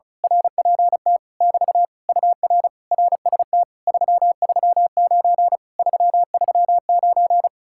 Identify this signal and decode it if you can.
Morse code, heard as a keyed tone, about 33 wpm, 700 Hz. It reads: RPT <BT> UR RST 339 339
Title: Speed-Racing Common QSO Phrases - 35wpm